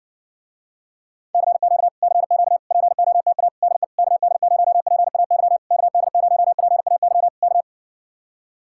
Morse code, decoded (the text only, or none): CQ CQ POTA DE KD0PAY KD0PAY K